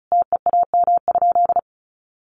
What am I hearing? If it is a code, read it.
Morse code, heard as a keyed tone, about 34 wpm, 700 Hz. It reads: TEAM?
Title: Sentences from Top 1500 Words 35wpm